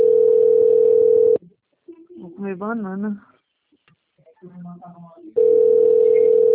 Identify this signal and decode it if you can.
Ringback tone (North America), 440 + 480 Hz, with speech.
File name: out- - - - - .22021